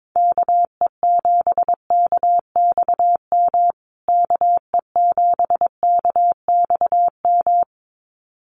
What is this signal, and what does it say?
Morse code, heard as a keyed tone, about 22 wpm, 700 Hz. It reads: KE7KXM KE7KXM
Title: USA Call Signs Repeated 2x 22wpm